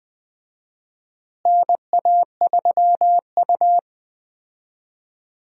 Morse code, heard as a keyed tone, about 20 wpm, 700 Hz. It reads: NA3U
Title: USA Call Signs - 4 Characters - Repeated 3x - 20wpm